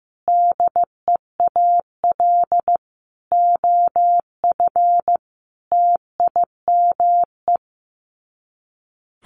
Morse code, heard as a keyed tone, about 15 wpm, 700 Hz. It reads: DEAL OF TIME
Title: Sets of 5 Words Encore2 15wpm